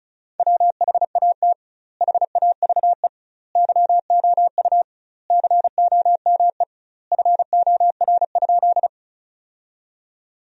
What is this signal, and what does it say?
Morse code, heard as a keyed tone, about 36 wpm, 700 Hz. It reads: WHAT HAVE YOU COME FOR?
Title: 2x Word-Spacing - Sentences from Top 100 Words - 35wpm